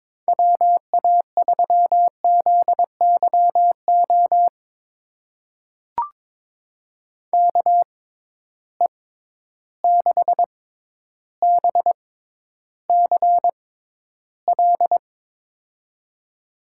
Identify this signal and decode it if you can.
Morse code, heard as a keyed tone, about 22 wpm, spaced as for 11 wpm, 700 Hz. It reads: WA3ZYO KE6BCL
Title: USA Call Signs - 6x Character Spacing - 22wpm